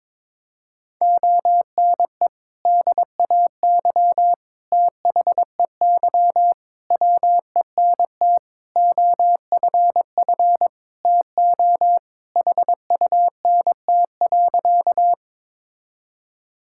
Morse code, heard as a keyed tone, about 22 wpm, 700 Hz. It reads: ONE DAY THEY WENT OFF TO HUNT.
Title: Sentences from Top 1600 Words 22wpm